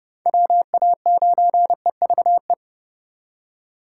Morse code, heard as a keyed tone, about 28 wpm, 700 Hz. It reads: WA9EVE